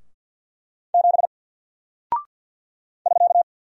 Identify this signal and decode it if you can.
Morse code, heard as a keyed tone, about 48 wpm, 700 Hz. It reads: 7 <SK>
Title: Single Character (Letter:Number:Punctuation:Prosign) Alt 50wpm